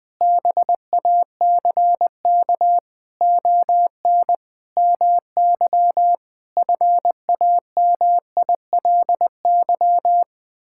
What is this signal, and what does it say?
Morse code, heard as a keyed tone, about 20 wpm, 700 Hz. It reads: BACK ON MY FAMILY
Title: Sentences from Top 1000 Words - Review Part 6 - 20wpm